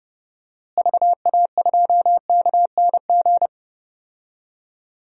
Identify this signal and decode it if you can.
Morse code, heard as a keyed tone, about 30 wpm, 700 Hz. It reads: VA2KNG
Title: DXCC-100 Call Signs - 6 Characters - Repeated 2x - 30wpm